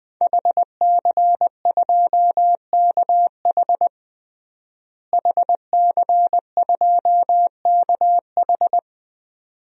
Morse code, heard as a keyed tone, about 20 wpm, 700 Hz. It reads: HC2KH HC2KH
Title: DXCC-100 Call Signs - 5 Characters - Repeated 2x - 20wpm